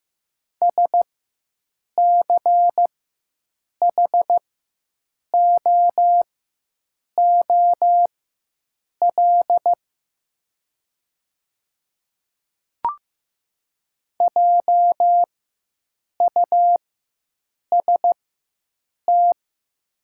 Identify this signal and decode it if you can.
Morse code, heard as a keyed tone, about 15 wpm, spaced as for 7 wpm, 700 Hz. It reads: SCHOOL JUST
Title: 4x Character Spacing - Top 300 Words - 15wpm